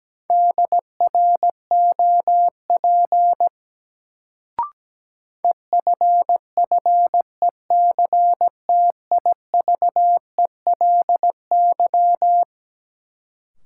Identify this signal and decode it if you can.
Morse code, heard as a keyed tone, about 17 wpm, 700 Hz. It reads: DROP EFFECTIVELY
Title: Sentences from Top 1400 Words - New Word Drill - 17wpm